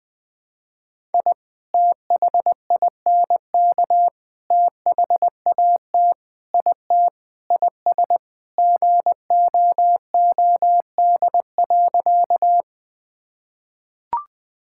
Morse code, heard as a keyed tone, about 21 wpm, 700 Hz. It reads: I THINK THAT IT IS GOOD.
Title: Sentences with 3-Word Phrases - 20wpm